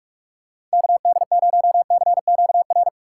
Morse code, heard as a keyed tone, about 46 wpm, 700 Hz. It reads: KD0CQR
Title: USA Call Signs - 6 Characters - 2x Character Spacing - 45wpm